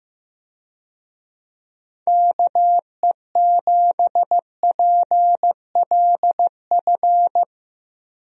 Morse code, heard as a keyed tone, about 15 wpm, 700 Hz. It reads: KE7PLF
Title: USA Call Signs - 6 Characters - Repeated 3x - 15wpm